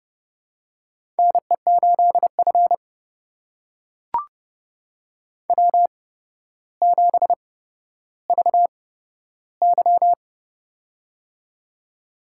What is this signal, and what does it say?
Morse code, heard as a keyed tone, about 32 wpm, spaced as for 13 wpm, 700 Hz. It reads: NE8F W7VY
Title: USA Call Signs - 4 Characters - 8x Character Spacing - 30wpm